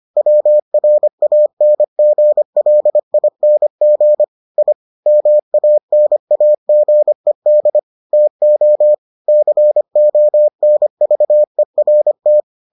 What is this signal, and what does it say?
Morse code, heard as a keyed tone, about 25 wpm, 600 Hz. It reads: WRANGLING I MANAGED TO CONVERT